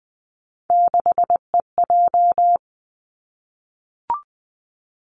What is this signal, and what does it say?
Morse code, heard as a keyed tone, about 20 wpm, 700 Hz. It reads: 6EJ
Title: Three Challenging Letter-Number Combinations 20wpm